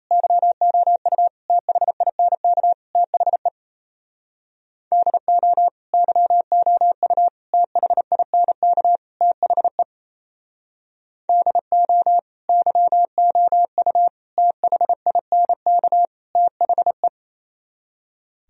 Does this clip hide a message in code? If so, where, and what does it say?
Yes: Morse code from 0.1 to 17.1 s, heard as a keyed tone, about 37 wpm, 700 Hz. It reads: YOU THINK THE DO YOU THINK THE DO YOU THINK THE